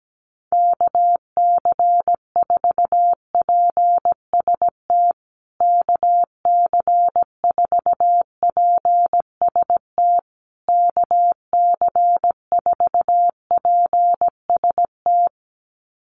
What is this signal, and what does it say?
Morse code, heard as a keyed tone, about 17 wpm, 700 Hz. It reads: KC4PST KC4PST KC4PST
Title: USA Call Signs Repeated 3x 17wpm